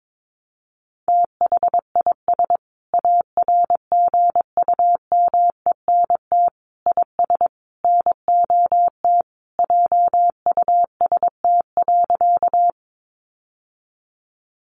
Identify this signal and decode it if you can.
Morse code, heard as a keyed tone, about 22 wpm, 700 Hz. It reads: THIS ARGUMENT IS NOT JUST.
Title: Sentences from Top 1400 Words 22wpm